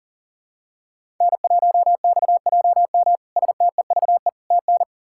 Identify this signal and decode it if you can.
Morse code, heard as a keyed tone, about 40 wpm, 700 Hz. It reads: N1XJM STEVE TN
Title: K1USN SST QSO Practice - 40wpm